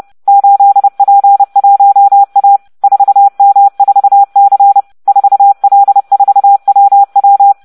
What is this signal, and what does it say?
Morse code, heard as a keyed tone, about 30 wpm, 800 Hz. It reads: 8P1A 4M4C 4L4WW